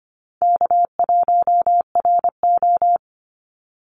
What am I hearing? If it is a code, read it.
Morse code, heard as a keyed tone, about 25 wpm, 700 Hz. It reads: K1RO
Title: Call Signs Repeated 3x - 25wpm